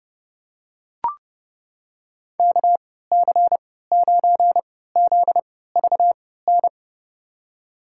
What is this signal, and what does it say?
Morse code, heard as a keyed tone, about 30 wpm, spaced as for 17 wpm, 700 Hz. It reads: KC9ZVN
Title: USA Call Signs - 6 Characters - 3x Character Spacing - 30wpm